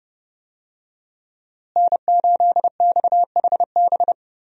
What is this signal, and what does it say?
Morse code, heard as a keyed tone, about 30 wpm, 700 Hz. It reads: N8XHB
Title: USA Call Signs - 5 Characters - Repeated 2x - 30wpm